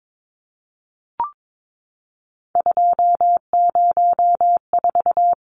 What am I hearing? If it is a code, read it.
Morse code, heard as a keyed tone, about 22 wpm, 700 Hz. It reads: 204
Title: Numbers - 3-Digit 22wpm